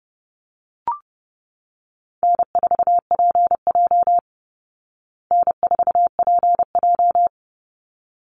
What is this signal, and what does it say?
Morse code, heard as a keyed tone, about 30 wpm, 700 Hz. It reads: N4PJ N4PJ
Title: Call Signs Repeated 3x - 30wpm